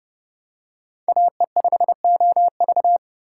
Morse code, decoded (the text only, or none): AE5OV